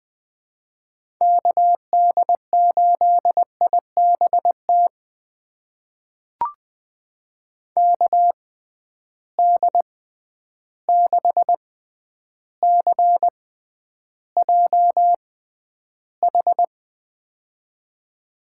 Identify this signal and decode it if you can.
Morse code, heard as a keyed tone, about 20 wpm, spaced as for 10 wpm, 700 Hz. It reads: KD8IBT KD6CJH
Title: USA Call Signs - 6 Characters - 6x Character Spacing - 20wpm